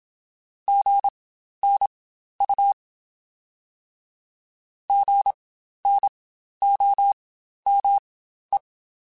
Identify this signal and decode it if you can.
Morse code, heard as a keyed tone, about 27 wpm, spaced as for 12 wpm, 800 Hz. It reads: GNU GNOME
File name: Spaced Alphas / Spaced Alphas mp3s / Gs sp @25